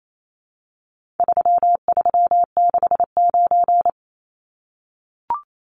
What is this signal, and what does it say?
Morse code, heard as a keyed tone, about 28 wpm, 700 Hz. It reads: 3369
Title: Numbers - 4-Digit 28wpm